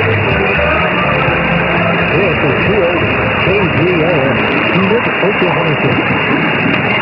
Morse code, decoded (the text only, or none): none